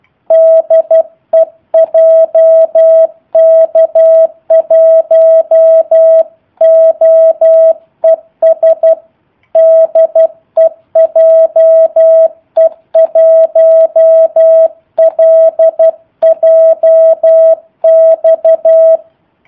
Morse code, heard as a keyed tone, about 12 wpm, 650 Hz. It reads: DE JK1OES DE JE1LJX
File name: jh8cbh / 2019cw / sp03- w